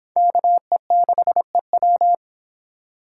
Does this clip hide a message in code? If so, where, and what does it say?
Yes: Morse code from 0.2 to 2.1 s, heard as a keyed tone, about 29 wpm, 700 Hz. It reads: KE6EW